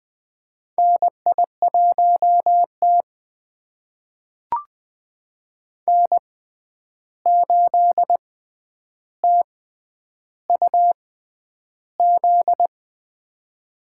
Morse code, heard as a keyed tone, about 20 wpm, spaced as for 10 wpm, 700 Hz. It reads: NI1T N8TUZ
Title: USA Call Signs - 6x Character Spacing - 20wpm